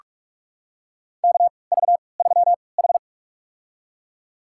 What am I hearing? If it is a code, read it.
Morse code, heard as a keyed tone, about 46 wpm, spaced as for 26 wpm, 700 Hz. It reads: KV3H
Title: USA Call Signs - 3x Character Spacing - 45wpm